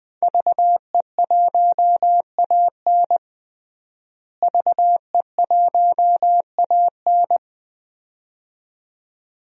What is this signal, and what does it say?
Morse code, heard as a keyed tone, about 20 wpm, 700 Hz. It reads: VE1AN VE1AN
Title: DXCC-100 Call Signs - 5 Characters - Repeated 2x - 20wpm